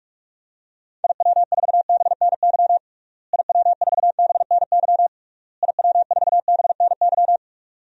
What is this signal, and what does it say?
Morse code, heard as a keyed tone, about 46 wpm, 700 Hz. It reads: IW4BNY IW4BNY IW4BNY
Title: DXCC-100 Call Signs - 6 Characters - Repeated 3x - 45wpm